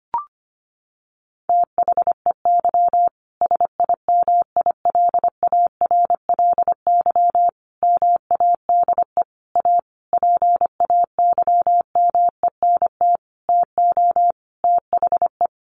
Morse code, heard as a keyed tone, about 25 wpm, 700 Hz. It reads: THEY SIMILARLY MADE A PAYMENT TO THE
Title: Sentences from Top 1400 Words 25wpm